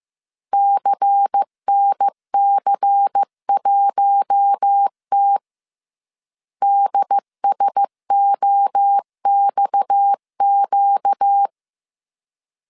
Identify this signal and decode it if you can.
Morse code, heard as a keyed tone, about 15 wpm, 800 Hz. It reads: CNC1T DSOXQ
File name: CW / DET / 1 6 1500C S A 14wpm 2921